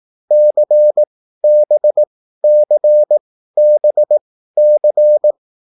Morse code, heard as a keyed tone, about 18 wpm, spaced as for 13 wpm, 600 Hz. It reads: CBCBC